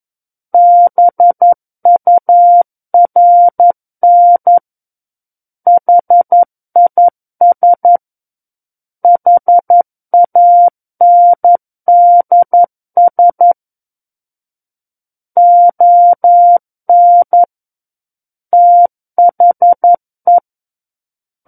Morse code, heard as a keyed tone, about 11 wpm, 700 Hz. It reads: BURN HIS HANDS ON THE
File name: 11 WPM